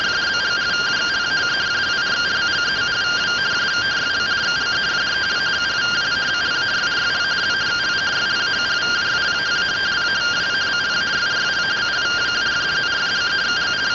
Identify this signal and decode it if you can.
Two-tone teleprinter signal, about 170 Hz shift.